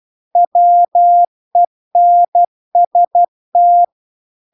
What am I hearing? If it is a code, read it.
Morse code, heard as a keyed tone, about 12 wpm, 700 Hz. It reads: WENST